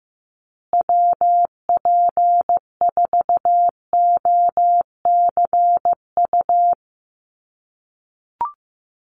Morse code, heard as a keyed tone, about 15 wpm, 700 Hz. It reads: WP4OCU
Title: USA Call Signs Repeated 3x 15wpm